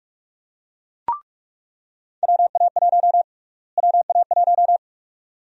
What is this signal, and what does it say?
Morse code, heard as a keyed tone, about 42 wpm, 700 Hz. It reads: WA1 WA1
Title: USA Call Sign Prefixes - Repeated 2x - 45wpm